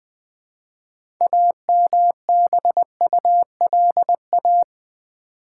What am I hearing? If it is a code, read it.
Morse code, heard as a keyed tone, about 20 wpm, 690 Hz. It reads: AMBULA
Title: Build A Word - 3 Letters At A Time - 9-Letter Words - 20wpm